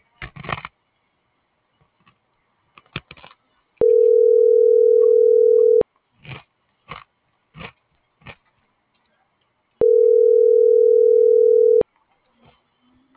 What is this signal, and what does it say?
Ringback tone (North America), 440 + 480 Hz.